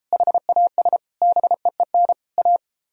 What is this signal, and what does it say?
Morse code, heard as a keyed tone, about 33 wpm, 700 Hz. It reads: HAS BEEN A